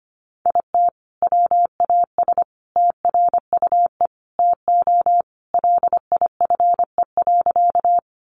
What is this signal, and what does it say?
Morse code, heard as a keyed tone, about 25 wpm, 700 Hz. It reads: IT WAS TRUE TO LIFE.
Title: Sentences from Top 400 Words 25wpm